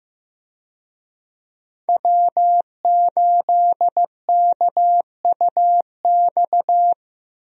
Morse code, heard as a keyed tone, about 15 wpm, 700 Hz. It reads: W8KUX
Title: USA Call Signs - 5 Characters - Repeated 2x - 15wpm